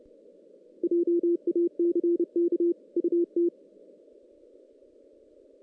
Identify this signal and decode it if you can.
Morse code, heard as a keyed tone, about 30 wpm, 340 Hz. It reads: JACK UT
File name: 2016 JAN CW NAQP / 2016 Jan naqp k5ot - 0-6